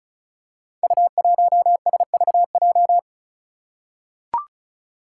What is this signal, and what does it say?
Morse code, heard as a keyed tone, about 36 wpm, 700 Hz. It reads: U1SVJ
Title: Five Challenging Letter-Number Combinations 35wpm